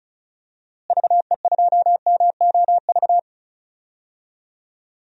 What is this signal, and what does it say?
Morse code, heard as a keyed tone, about 34 wpm, 700 Hz. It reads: VE2MOV
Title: DXCC-100 Call Signs - 6 Characters - Repeated 3x - 35wpm